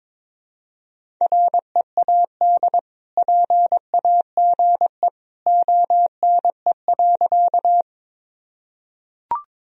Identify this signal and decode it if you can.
Morse code, heard as a keyed tone, about 22 wpm, 700 Hz. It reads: READ PAGE ONE.
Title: Sentences from Top 1000 Words - Review Part 1 - 22wpm